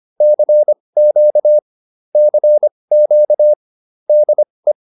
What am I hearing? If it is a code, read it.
Morse code, heard as a keyed tone, about 25 wpm, spaced as for 20 wpm, 600 Hz. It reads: CQ CQ DE
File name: QSO105a 20